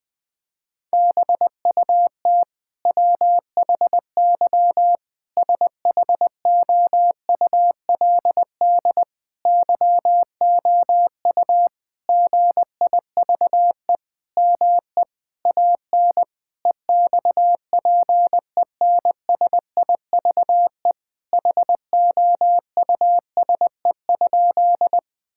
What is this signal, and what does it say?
Morse code, heard as a keyed tone, about 20 wpm, 700 Hz. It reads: BUT WHY SHOULD YOU GIVE ME AN EXPENSIVE HOUSE?
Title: Sentences from Top 1000 Words - Review Part 6 - 20wpm